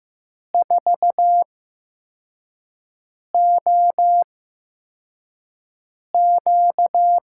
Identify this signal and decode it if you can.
Morse code, heard as a keyed tone, about 15 wpm, spaced as for 4 wpm, 700 Hz. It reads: 4OQ
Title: USA Call Signs - 5 Characters - 8x Character Spacing - 15wpm